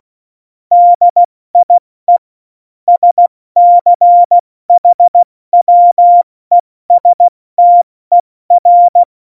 Morse code, heard as a keyed tone, about 16 wpm, 700 Hz. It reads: DIE SCHWESTER
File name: Morsekurs / Witze / Witz 29. September 2020, 70 F80